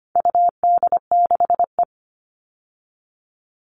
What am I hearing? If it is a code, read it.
Morse code, heard as a keyed tone, about 25 wpm, 700 Hz. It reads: UD6E
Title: All Cut Numbers - 4-Digit 25wpm